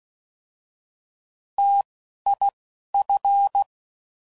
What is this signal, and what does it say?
Morse code, heard as a keyed tone, about 16 wpm, 800 Hz. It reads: T I F